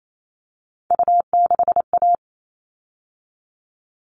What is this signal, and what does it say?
Morse code, heard as a keyed tone, about 28 wpm, 700 Hz. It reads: U6A
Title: All Cut Numbers - 3-Digit 28wpm